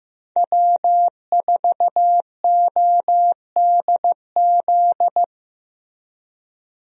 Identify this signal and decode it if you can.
Morse code, heard as a keyed tone, about 15 wpm, 700 Hz. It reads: W4ODZ
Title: USA Call Signs - 5 Characters - Repeated 3x - 15wpm